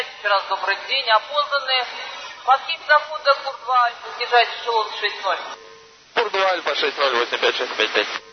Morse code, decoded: none

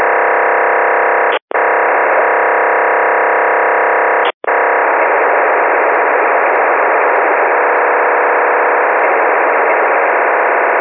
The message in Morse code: none

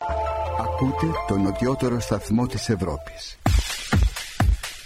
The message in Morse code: none

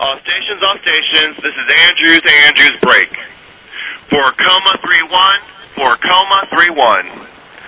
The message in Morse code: none

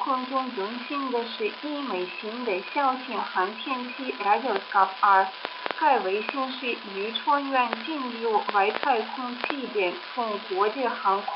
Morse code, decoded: none

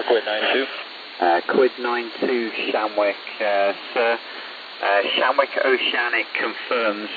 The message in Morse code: none

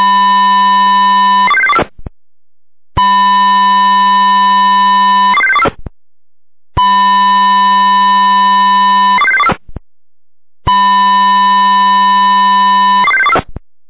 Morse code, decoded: none

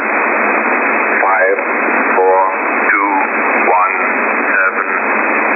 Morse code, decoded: none

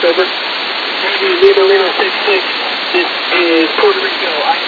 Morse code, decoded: none

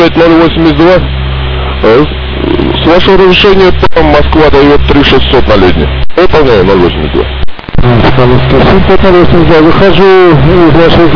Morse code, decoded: none

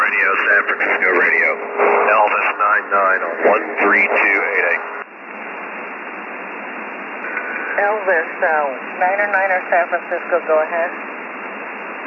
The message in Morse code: none

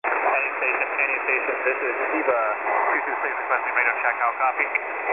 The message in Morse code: none

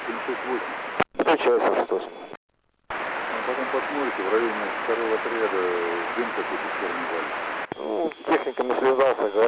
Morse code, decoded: none

